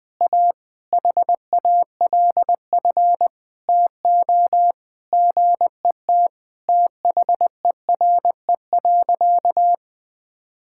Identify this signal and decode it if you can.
Morse code, heard as a keyed tone, about 20 wpm, 700 Hz. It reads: A HALF TO GET THERE.
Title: Sentences with 5-Word Phrases - 20wpm